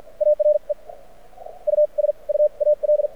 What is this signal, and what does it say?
Morse code, heard as a keyed tone, about 55 wpm, spaced as for 40 wpm, 600 Hz. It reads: ARE USUAL
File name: archived / CRSnet